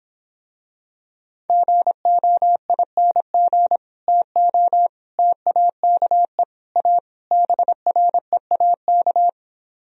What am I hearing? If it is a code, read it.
Morse code, heard as a keyed tone, about 26 wpm, 700 Hz. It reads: GOING TO TAKE A BREAK